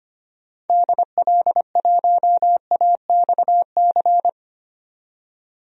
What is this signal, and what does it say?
Morse code, heard as a keyed tone, about 25 wpm, 700 Hz. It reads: DL1AXC
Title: DXCC-100 Call Signs - 6 Characters - 25wpm